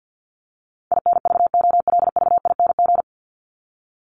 Morse code, heard as a keyed tone, about 50 wpm, 700 Hz. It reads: INVOLVING
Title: ING Suffix - Top 100 Words with Base Words 50wpm